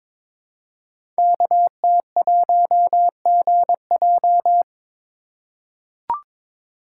Morse code, heard as a keyed tone, about 21 wpm, 700 Hz. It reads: KT1GJ
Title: Code Groups - 5 Letter:Numbers - 22wpm